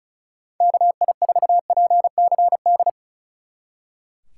Morse code, heard as a keyed tone, about 46 wpm, 700 Hz. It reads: KI4PCD